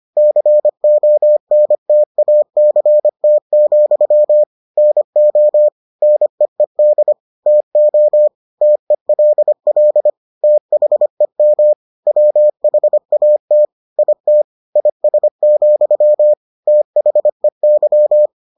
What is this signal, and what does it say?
Morse code, heard as a keyed tone, about 25 wpm, 600 Hz. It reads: CONTACT, NO NEED TO TELL THEM WHAT IT IS, THEY